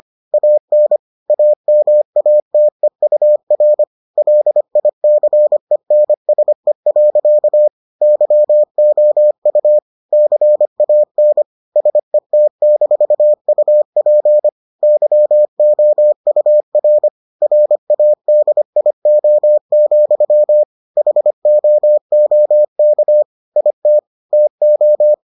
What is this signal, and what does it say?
Morse code, heard as a keyed tone, about 25 wpm, 600 Hz. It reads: AN AMATEUR LICENSE. YOU CAN SET-UP YOUR RADIO, HOOK IT TO